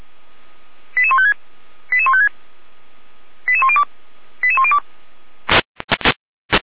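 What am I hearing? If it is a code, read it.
Five-tone selective call (ZVEI), heard as quick runs of five tones, tones of about 70 ms. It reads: 88156 88156 88102 88102
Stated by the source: ZVEI five-tone selective call received by SDR.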